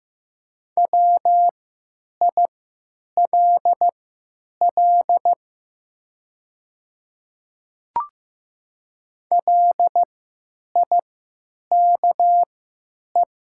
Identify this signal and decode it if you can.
Morse code, heard as a keyed tone, about 15 wpm, spaced as for 8 wpm, 700 Hz. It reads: WILL LIKE
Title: 3x Character Spacing - Top 100 Words - 15wpm